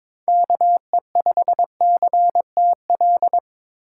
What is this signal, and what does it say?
Morse code, heard as a keyed tone, about 22 wpm, 700 Hz. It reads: KE5CTL
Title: USA Call Signs - 6 Characters - 8x Character Spacing - 22wpm